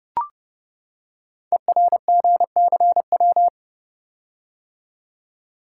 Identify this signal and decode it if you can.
Morse code, heard as a keyed tone, about 30 wpm, 700 Hz. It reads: ERGCW
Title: Code Groups - 5 Letter:Numbers - 30wpm